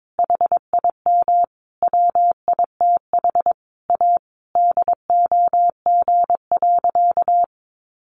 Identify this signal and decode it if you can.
Morse code, heard as a keyed tone, about 22 wpm, 700 Hz. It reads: HIM WITH A DOG.
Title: Additional Easy Phrases 22wpm